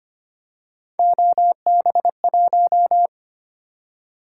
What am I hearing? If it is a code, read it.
Morse code, heard as a keyed tone, about 25 wpm, 700 Hz. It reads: OB1
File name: Three Letter-Number Combinations - 25wpm